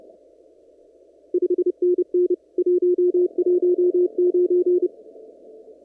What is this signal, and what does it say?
Morse code, heard as a keyed tone, about 30 wpm, 360 Hz. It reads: 5NN 119
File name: CW / kt5j wpx cw-op-k5pi